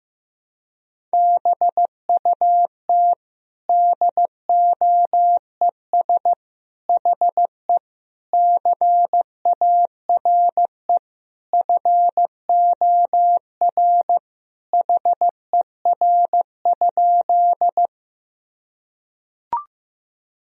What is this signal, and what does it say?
Morse code, heard as a keyed tone, about 15 wpm, 700 Hz. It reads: BUT DOES HE CARE FOR HER?
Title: Sentences from Top 1000 Words - Review Part 1 - 15wpm